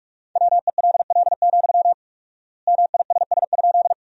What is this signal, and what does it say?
Morse code, heard as a keyed tone, about 46 wpm, 700 Hz. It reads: WELL, MISS?
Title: 4x Word-Spacing - Sentences from Top 500 Words - 45wpm